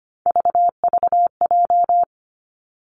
Morse code, heard as a keyed tone, about 25 wpm, 700 Hz. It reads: VVJ
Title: Three Challenging Letter-Number Combinations 25wpm